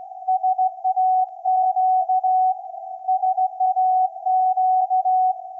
Morse code, heard as a keyed tone, about 17 wpm, 750 Hz. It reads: SAQ SAQ
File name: iz7slz / VLF / SAQ 24-12-2024